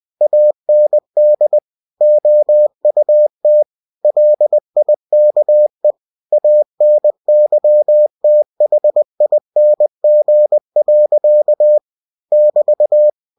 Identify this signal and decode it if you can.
Morse code, heard as a keyed tone, about 20 wpm, 600 Hz. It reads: AND OUT LIKE ANYTHING. <BT>
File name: cw-books / winnie-the-pooh / winnie-20wpm-page-0058